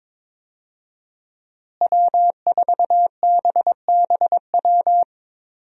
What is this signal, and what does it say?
Morse code, heard as a keyed tone, about 21 wpm, 700 Hz. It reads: W4BBW
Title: USA Call Signs - 5 Characters - Repeated 3x - 22wpm